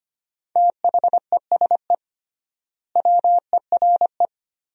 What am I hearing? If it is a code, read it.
Morse code, heard as a keyed tone, about 25 wpm, 700 Hz. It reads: THESE WERE
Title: 3x Word-Spacing - Sentences from Top 300 Words - 25wpm